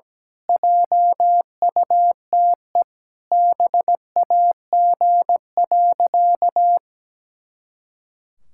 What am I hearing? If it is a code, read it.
Morse code, heard as a keyed tone, about 17 wpm, 700 Hz. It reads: JUTE BAG.
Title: Pangram Challenge v2 - 17wpm